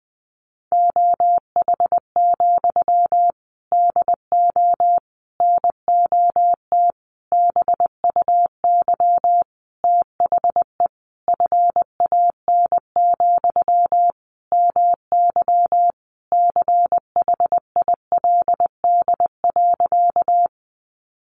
Morse code, heard as a keyed tone, about 20 wpm, 700 Hz. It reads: OH, DO NOT BUY THE FAN, MY CHILD.
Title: Sentences from Top 1300 Words 20wpm